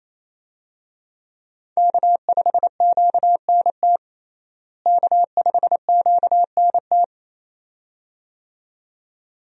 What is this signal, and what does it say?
Morse code, heard as a keyed tone, about 28 wpm, 700 Hz. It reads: K5QNT K5QNT
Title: USA Call Signs - 5 Characters - Repeated 2x - 28wpm